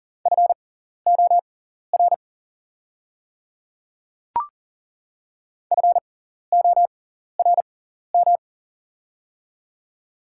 Morse code, heard as a keyed tone, about 40 wpm, spaced as for 14 wpm, 700 Hz. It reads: FOR FORM
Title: 6x Character Spacing - Top 200 Words - 40wpm